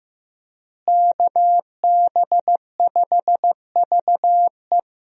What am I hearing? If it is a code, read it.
Morse code, heard as a keyed tone, about 15 wpm, 700 Hz. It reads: KB5VE
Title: USA Call Signs - 5 Characters - Repeated 3x - 15wpm